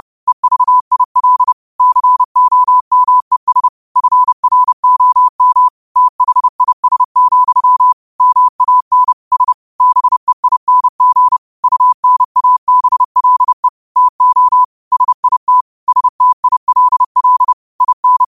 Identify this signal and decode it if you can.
Morse code, heard as a keyed tone, about 30 wpm, 1000 Hz. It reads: EVIL COMES FROM THIS, MAN S BEING UNABLE TO SIT STILL IN
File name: qotdfiles / 30wpm